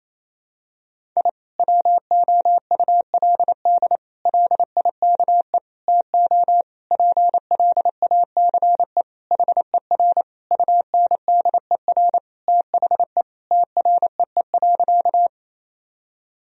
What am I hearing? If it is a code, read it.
Morse code, heard as a keyed tone, about 28 wpm, 700 Hz. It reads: I WOULD LIKE TO PLACE HER UNDER THE TREE.
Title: Sentences from Top 1000 Words - Review Part 1 - 28wpm